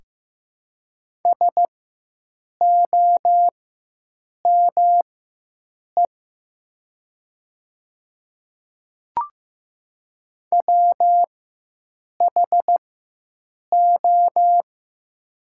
Morse code, heard as a keyed tone, about 15 wpm, spaced as for 7 wpm, 700 Hz. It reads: SOME WHO
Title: 4x Character Spacing - Top 200 Words - 15wpm